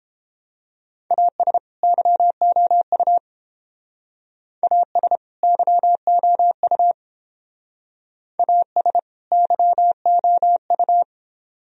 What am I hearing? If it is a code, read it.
Morse code, heard as a keyed tone, about 28 wpm, 700 Hz. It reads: AS YOU AS YOU AS YOU